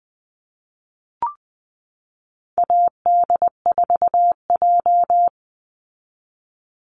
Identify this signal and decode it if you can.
Morse code, heard as a keyed tone, about 19 wpm, 700 Hz. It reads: AD4J
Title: Call Signs Repeated 3x - 20wpm